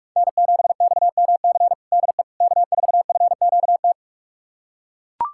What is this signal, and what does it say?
Morse code, heard as a keyed tone, about 41 wpm, 690 Hz. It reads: N7XMC DE K4FQT
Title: USA Callsign DE Callsign 45wpm